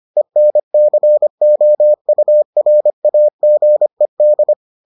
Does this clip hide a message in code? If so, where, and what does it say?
Yes: Morse code from 0.2 to 4.5 s, heard as a keyed tone, about 25 wpm, 600 Hz. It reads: ENCOURAGED